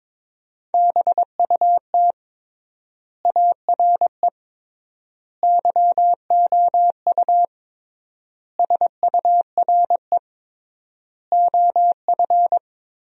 Morse code, heard as a keyed tone, about 22 wpm, 700 Hz. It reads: BUT ARE YOU SURE OF
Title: 3x Word-Spacing - Sentences from Top 300 Words - 22wpm